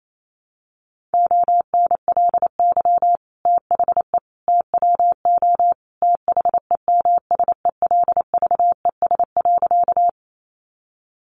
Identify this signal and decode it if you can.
Morse code, heard as a keyed tone, about 28 wpm, 700 Hz. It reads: ONLY THE TWO THEMSELVES.
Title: Sentences from Top 800 Words 28wpm